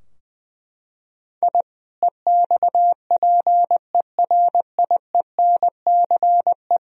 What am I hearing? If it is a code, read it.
Morse code, heard as a keyed tone, about 20 wpm, 700 Hz. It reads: I EXPERIENCE
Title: Sentences from Top 1000 Words - Review Part 3 - 20wpm